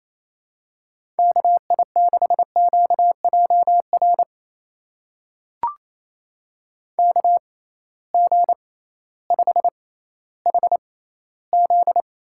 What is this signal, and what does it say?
Morse code, heard as a keyed tone, about 28 wpm, spaced as for 15 wpm, 700 Hz. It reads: KI6QJR KG5HZ
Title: USA Call Signs - 6 Characters - 6x Character Spacing - 28wpm